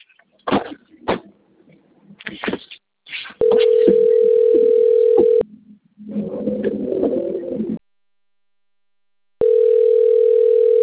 Ringback tone (North America), 440 + 480 Hz.
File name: out- - - - - .5236